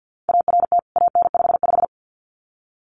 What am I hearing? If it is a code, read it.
Morse code, heard as a keyed tone, about 50 wpm, 700 Hz. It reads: ART AN55
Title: CWT Exchange 50wpm